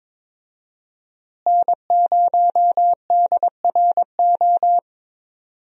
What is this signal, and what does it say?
Morse code, heard as a keyed tone, about 22 wpm, 700 Hz. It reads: N0DRO
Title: USA Call Signs - 5 Characters - Repeated 3x - 22wpm